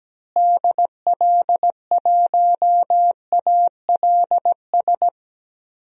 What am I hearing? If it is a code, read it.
Morse code, heard as a keyed tone, about 17 wpm, 700 Hz. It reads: DL1ALS
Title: DXCC-100 Call Signs - 6 Characters - Repeated 3x - 17wpm